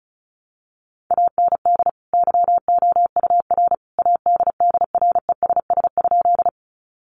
Morse code, heard as a keyed tone, about 34 wpm, 700 Hz. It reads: AND YOUR ADDRESS?
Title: Sentences from Top 1000 Words 35wpm